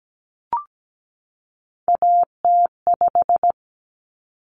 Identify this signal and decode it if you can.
Morse code, heard as a keyed tone, about 17 wpm, 690 Hz. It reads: AT5
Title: Common Cut Numbers - 3-Digit 17wpm